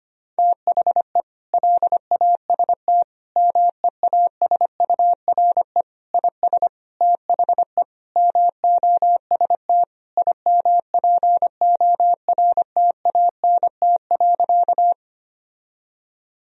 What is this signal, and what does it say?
Morse code, heard as a keyed tone, about 25 wpm, 700 Hz. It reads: THE LAST MEASURE IS THE MOST IMPORTANT.
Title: Sentences from Top 1000 Words - Review Part 2 - 25wpm